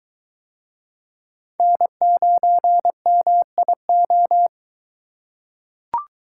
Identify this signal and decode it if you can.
Morse code, heard as a keyed tone, about 22 wpm, 700 Hz. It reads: N9MIO